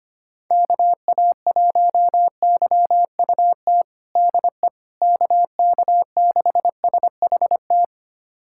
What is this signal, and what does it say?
Morse code, heard as a keyed tone, about 25 wpm, 700 Hz. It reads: KA1YUT DE KK6SHT
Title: USA Callsign DE Callsign 25wpm